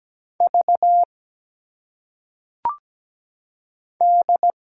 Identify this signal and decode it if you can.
Morse code, heard as a keyed tone, about 17 wpm, 700 Hz. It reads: V D
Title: Single Letters - Phonetic Pronunciation 17wpm